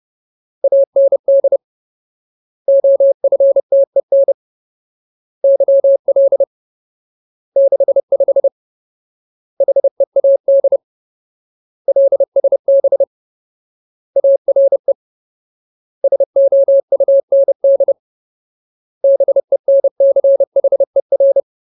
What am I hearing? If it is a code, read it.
Morse code, heard as a keyed tone, about 30 wpm, 550 Hz. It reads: AND OFTEN YL 65 HEAD LSB ARE SOUND BENCHER
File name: Ultimate Word List / 30WPM Ultimate